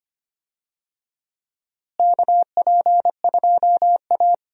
Morse code, heard as a keyed tone, about 25 wpm, 700 Hz. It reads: KP2A
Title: DXCC-100 Call Signs - 4 Characters - Repeated 3x - 25wpm